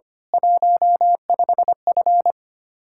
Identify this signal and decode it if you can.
Morse code, heard as a keyed tone, about 27 wpm, 700 Hz. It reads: 15F
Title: Speed-Racing Common QSO Phrases - 25wpm